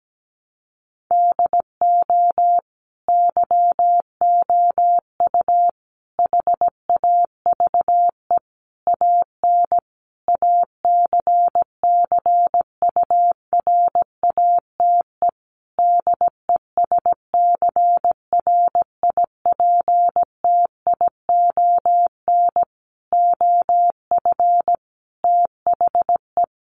Morse code, heard as a keyed tone, about 17 wpm, 700 Hz. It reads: DO YOU HAVE AN ACCURATE DESCRIPTION OF THE
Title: Sentences from Top 1300 Words 17wpm